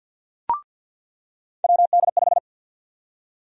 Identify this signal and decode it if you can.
Morse code, heard as a keyed tone, about 48 wpm, 700 Hz. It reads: WD5
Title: USA Call Sign Prefixes - 50wpm